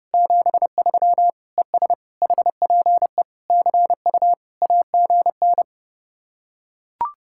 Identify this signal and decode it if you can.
Morse code, heard as a keyed tone, about 31 wpm, 700 Hz. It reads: 73 ES HPE CU AGN